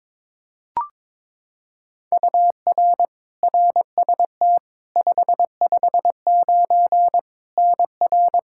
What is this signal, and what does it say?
Morse code, heard as a keyed tone, about 21 wpm, 700 Hz. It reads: UR RST 559 NR
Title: K1USN SST QSO Practice 22wpm